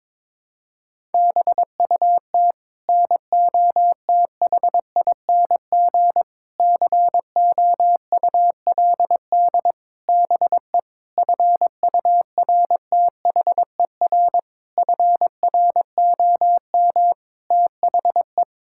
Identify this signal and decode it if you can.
Morse code, heard as a keyed tone, about 22 wpm, 700 Hz. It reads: BUT NOTHING COULD BE FURTHER FROM THE
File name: practice / 22wpm / Sets of 8 Words - 22wpm